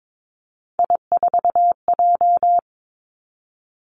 Morse code, heard as a keyed tone, about 22 wpm, 700 Hz. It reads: I4J
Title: Three Challenging Letter-Number Combinations 22wpm